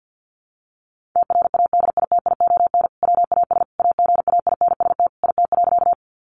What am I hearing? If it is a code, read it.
Morse code, heard as a keyed tone, about 50 wpm, 700 Hz. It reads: TRADITION WAS AGAINST IT.
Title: Sentences from Top 1500 Words 50wpm